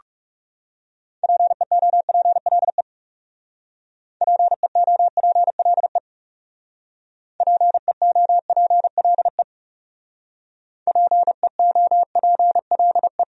Morse code, heard as a keyed tone, about 34 wpm, 700 Hz. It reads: PEOPLE PEOPLE PEOPLE PEOPLE
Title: Speed-Racing Top 300 Words 30wpm